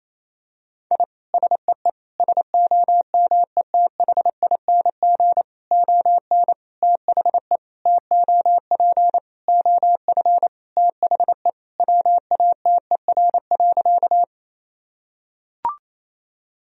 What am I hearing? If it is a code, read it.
Morse code, heard as a keyed tone, about 28 wpm, 700 Hz. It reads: I SEE SOMETHING ON THE TOP OF THE WATER.
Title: Sentences from Top 1000 Words - Review Part 2 - 28wpm